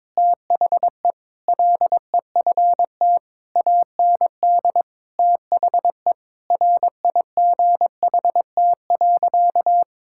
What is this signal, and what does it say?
Morse code, heard as a keyed tone, about 22 wpm, 700 Hz. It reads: THE LEFT AND THE RIGHT.
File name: Sentences with 7-Word Phrases - 22wpm